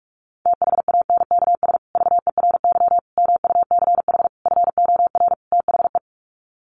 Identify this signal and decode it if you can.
Morse code, heard as a keyed tone, about 44 wpm, 700 Hz. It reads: THANKS VERY MUCH FOR THE
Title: Sets of 5 Words Encore2 45wpm